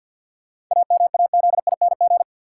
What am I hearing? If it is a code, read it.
Morse code, heard as a keyed tone, about 52 wpm, 700 Hz. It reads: AMAZING
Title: Sentences from Top 1500 Words - New Word Drill - 50wpm